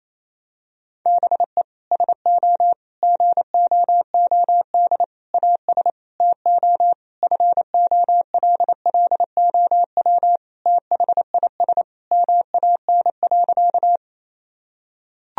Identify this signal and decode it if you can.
Morse code, heard as a keyed tone, about 28 wpm, 700 Hz. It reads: BE SO GOOD AS TO FOLLOW THIS MAN.
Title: Sentences from Top 1000 Words - Review Part 1 - 28wpm